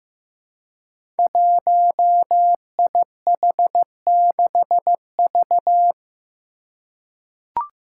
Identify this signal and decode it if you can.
Morse code, heard as a keyed tone, about 15 wpm, 700 Hz. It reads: 1IH6V
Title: Five Challenging Letter-Number Combinations 15wpm